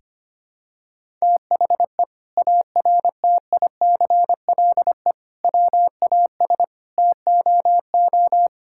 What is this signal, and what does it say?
Morse code, heard as a keyed tone, about 25 wpm, 700 Hz. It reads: THE ARTICLE WAS TOO
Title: Sentences from Top 1000 Words - Review Part 5 - 25wpm